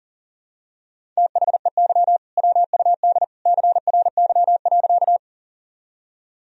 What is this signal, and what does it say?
Morse code, heard as a keyed tone, about 40 wpm, 700 Hz. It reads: THEY WUD CRY.
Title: Sentences from Top 500 Words and Abbreviations 40wpm